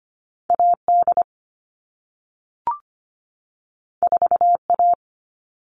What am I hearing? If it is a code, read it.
Morse code, heard as a keyed tone, about 25 wpm, 700 Hz. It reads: AD 4A
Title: All Cut Numbers - 2-Digit 25wpm